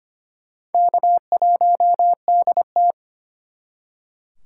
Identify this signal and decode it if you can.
Morse code, heard as a keyed tone, about 32 wpm, 700 Hz. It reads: K1DT